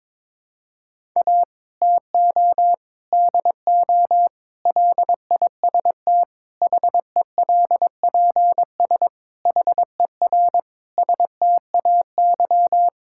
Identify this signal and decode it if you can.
Morse code, heard as a keyed tone, about 22 wpm, 700 Hz. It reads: A TO DO LIST HELPS HER STAY
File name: Difficult Sentences with Hints - 22wpm